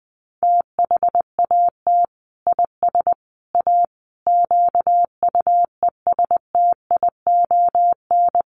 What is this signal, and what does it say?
Morse code, heard as a keyed tone, about 20 wpm, 700 Hz. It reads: THAT IS A QUESTION
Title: Sentences from Top 600 Words 20wpm